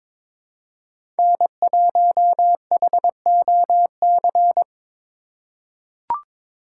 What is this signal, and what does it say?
Morse code, heard as a keyed tone, about 22 wpm, 700 Hz. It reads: N1HOC
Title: USA Call Signs - 5 Characters - 3x Character Spacing - 22wpm